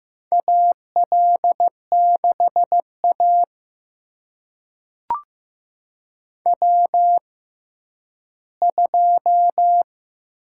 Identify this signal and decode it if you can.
Morse code, heard as a keyed tone, about 15 wpm, spaced as for 10 wpm, 700 Hz. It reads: AL6A W2
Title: USA Call Signs - 4 Characters - 6x Character Spacing - 15wpm